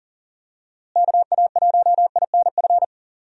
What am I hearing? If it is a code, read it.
Morse code, heard as a keyed tone, about 40 wpm, 700 Hz. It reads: KA1INF
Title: USA Call Signs - 6 Characters - Repeated 3x - 40wpm